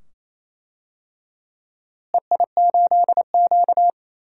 Morse code, heard as a keyed tone, about 27 wpm, 700 Hz. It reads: EI8Q
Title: DXCC-100 Call Signs - 4 Characters - Repeated 2x - 28wpm